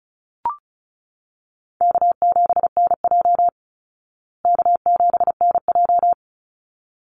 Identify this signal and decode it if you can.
Morse code, heard as a keyed tone, about 34 wpm, 700 Hz. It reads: K7NJ K7NJ
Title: Easy Call Signs Repeated 2x - 35wpm